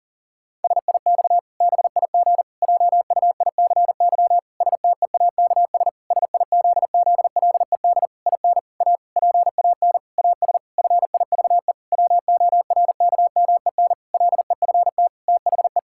Morse code, heard as a keyed tone, about 40 wpm, 700 Hz. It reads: SIX BIG JUICY STEAKS SIZZLED IN A PAN AS FIVE WORKMEN LEFT THE
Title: Pangram Challenge v2 - 40wpm